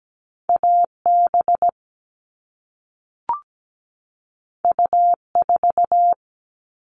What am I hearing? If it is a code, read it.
Morse code, heard as a keyed tone, about 17 wpm, 700 Hz. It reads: AB U4
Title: All Cut Numbers - 2-Digit 17wpm